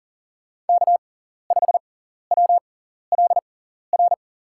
Morse code, heard as a keyed tone, about 38 wpm, spaced as for 14 wpm, 700 Hz. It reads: K5WLR
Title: USA Call Signs - 5 Characters - 6x Character Spacing - 40wpm